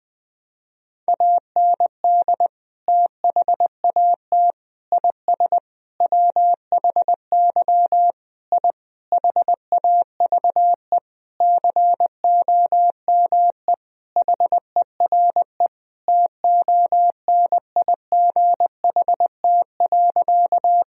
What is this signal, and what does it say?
Morse code, heard as a keyed tone, about 20 wpm, 700 Hz. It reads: AND THAT IS WHY I HAVE COME HERE TONIGHT.
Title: Sentences from Top 1000 Words - Review Part 3 - 20wpm